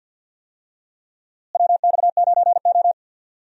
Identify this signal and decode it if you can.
Morse code, heard as a keyed tone, about 52 wpm, 700 Hz. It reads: WX9O
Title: USA Call Signs - 4 Characters - Repeated 2x - 50wpm